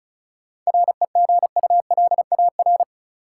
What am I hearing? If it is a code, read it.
Morse code, heard as a keyed tone, about 34 wpm, 700 Hz. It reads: REGULAR
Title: Sentences from Top 900 Words - New Word Drill - 35wpm